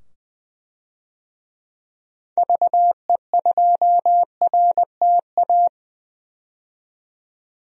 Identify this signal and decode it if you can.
Morse code, heard as a keyed tone, about 20 wpm, 700 Hz. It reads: VE2RTA
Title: DXCC-100 Call Signs - 6 Characters - 20wpm